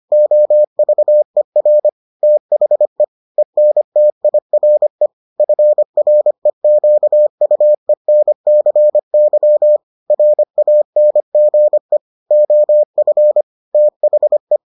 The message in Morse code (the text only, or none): OVER THE ENTIRE FREQUENCY RANGE OF THE